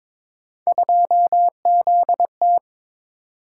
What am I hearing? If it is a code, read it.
Morse code, heard as a keyed tone, about 21 wpm, 700 Hz. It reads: 2ZT
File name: practice / 22wpm / Three Letter-Number Combinations - 22wpm